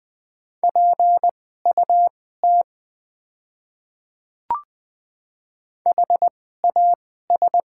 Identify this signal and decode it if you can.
Morse code, heard as a keyed tone, about 20 wpm, spaced as for 14 wpm, 700 Hz. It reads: PUT HAS
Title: 2x Character Spacing - Top 500 Words - 20wpm